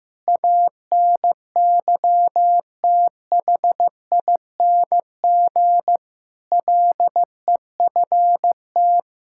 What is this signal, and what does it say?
Morse code, heard as a keyed tone, about 16 wpm, 700 Hz. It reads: ANYTHING LEFT
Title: Sentences from Top 1000 Words - Review Part 6 - 15wpm